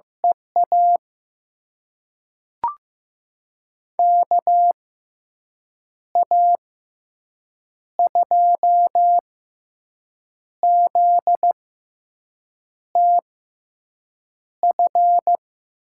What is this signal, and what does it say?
Morse code, heard as a keyed tone, about 15 wpm, spaced as for 6 wpm, 700 Hz. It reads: EA KA2ZTF
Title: USA Call Signs - 6 Characters - 6x Character Spacing - 15wpm